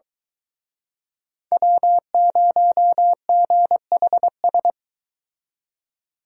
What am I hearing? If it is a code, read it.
Morse code, heard as a keyed tone, about 23 wpm, 700 Hz. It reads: W0GHS